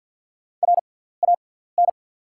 Morse code, heard as a keyed tone, about 51 wpm, spaced as for 17 wpm, 690 Hz. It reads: RAN
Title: 6x Character Spacing - Top 500 Words - 50wpm